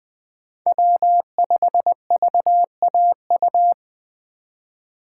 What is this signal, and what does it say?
Morse code, heard as a keyed tone, about 20 wpm, 700 Hz. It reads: W5VAU
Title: USA Call Signs - 2x Character Spacing - 20wpm